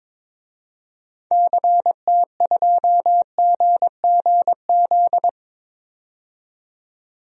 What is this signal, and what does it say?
Morse code, heard as a keyed tone, about 22 wpm, 700 Hz. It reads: CT2GGZ
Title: DXCC-100 Call Signs - 6 Characters - 22wpm